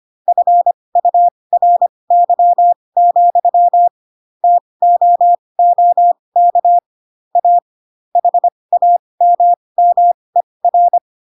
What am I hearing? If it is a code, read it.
Morse code, heard as a keyed tone, about 25 wpm, spaced as for 20 wpm, 700 Hz. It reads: FURY, TOOK A HAMMER
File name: Pinocchio / Pinocchio ch4 15 25 ik1hge